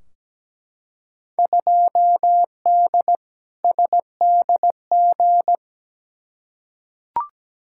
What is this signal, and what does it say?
Morse code, heard as a keyed tone, about 17 wpm, 700 Hz. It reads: 2D SDG
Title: Field Day Exchanges Encore - 17wpm